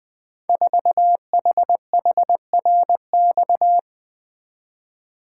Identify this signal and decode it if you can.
Morse code, heard as a keyed tone, about 20 wpm, 700 Hz. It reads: 4HHRX
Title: Code Groups - 5 Letter:Numbers - 20wpm